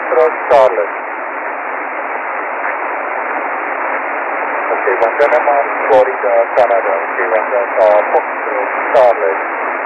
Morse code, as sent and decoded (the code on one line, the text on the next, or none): none